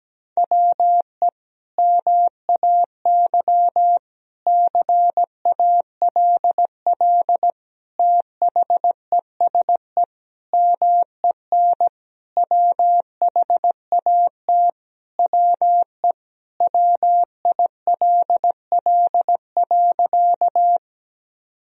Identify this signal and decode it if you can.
Morse code, heard as a keyed tone, about 17 wpm, 700 Hz. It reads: WE MAY CALL THESE MEN WHAT WE WILL.
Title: Sentences from Top 1000 Words - Review Part 1 - 17wpm